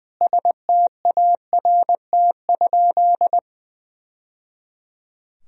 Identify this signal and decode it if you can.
Morse code, heard as a keyed tone, about 20 wpm, 700 Hz. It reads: START?
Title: Sentences from Top 1000 Words - Review Part 1 - 20wpm